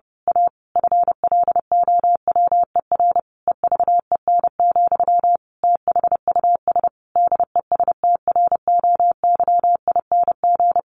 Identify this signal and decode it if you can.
Morse code, heard as a keyed tone, about 30 wpm, 700 Hz. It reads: A FLOWER EVEN, THUS DESTROYING
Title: A Sound Of Thunder 30wpm